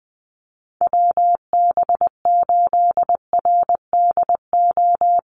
Morse code, heard as a keyed tone, about 20 wpm, 700 Hz. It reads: WB8RDO
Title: USA Call Signs Repeated 2x 20wpm